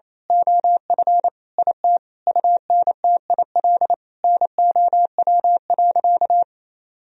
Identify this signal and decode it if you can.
Morse code, heard as a keyed tone, about 28 wpm, 700 Hz. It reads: OF IT UNTIL NOW.
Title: Sentences from Top 1000 Words - Review Part 1 - 28wpm